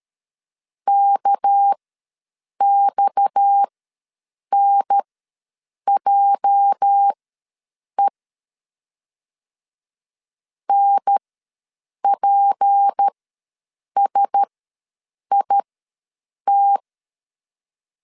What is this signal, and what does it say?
Morse code, heard as a keyed tone, about 13 wpm, spaced as for 7 wpm, 800 Hz. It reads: KXNJE NPSIT
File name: LEZIONI / 8 YJ45 / 8 YJ45 ws 12wpm 3146